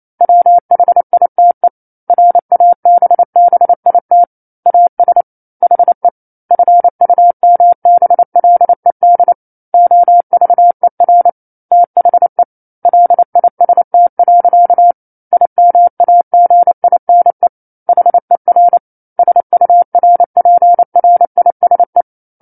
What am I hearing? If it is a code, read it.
Morse code, heard as a keyed tone, about 28 wpm, 700 Hz. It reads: WHITE RABBIT AS HE FUMBLED OVER THE LIST. IMAGINE HER SURPRISE
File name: 09-Chapter-09 28 WPM